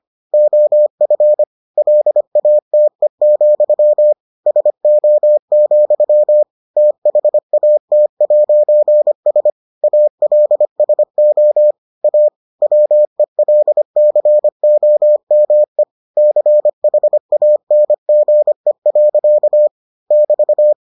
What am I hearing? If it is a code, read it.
Morse code, heard as a keyed tone, about 25 wpm, 600 Hz. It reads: OF LATE, SO, THAT'S ALSO A WELCOME CHANGE. <BT>